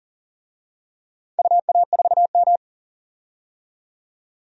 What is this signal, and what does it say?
Morse code, heard as a keyed tone, about 40 wpm, 700 Hz. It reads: UA4M
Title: DXCC-100 Call Signs - 4 Characters - 40wpm